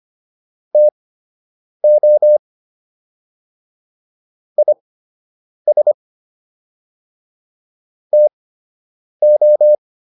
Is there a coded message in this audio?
Yes: Morse code from 0.7 to 9.7 s, heard as a keyed tone, about 25 wpm, spaced as for 8 wpm, 600 Hz. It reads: TO IS TO